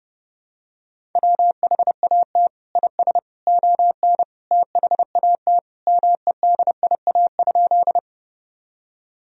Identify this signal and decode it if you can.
Morse code, heard as a keyed tone, about 32 wpm, 700 Hz. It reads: WHAT IS ON THAT MEDIA?
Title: Sentences from Top 1000 Words - Review Part 4 - 30wpm